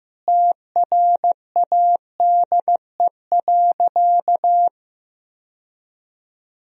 Morse code, heard as a keyed tone, about 15 wpm, 700 Hz. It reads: TRADE.
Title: Sentences from Top 1000 Words - Review Part 4 - 15wpm